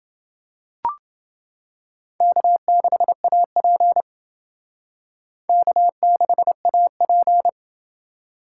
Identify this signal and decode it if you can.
Morse code, heard as a keyed tone, about 28 wpm, 700 Hz. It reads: K6AP K6AP